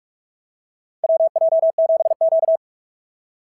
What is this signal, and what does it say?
Morse code, heard as a keyed tone, about 45 wpm, 690 Hz. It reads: WJ7Q
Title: SOTA QSO Practice 45wpm